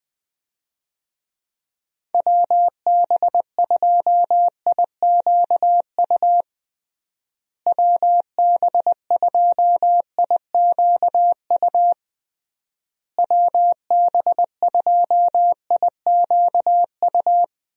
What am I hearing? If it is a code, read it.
Morse code, heard as a keyed tone, about 20 wpm, 700 Hz. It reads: WB2IQU WB2IQU WB2IQU
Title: USA Call Signs - 6 Characters - Repeated 3x - 20wpm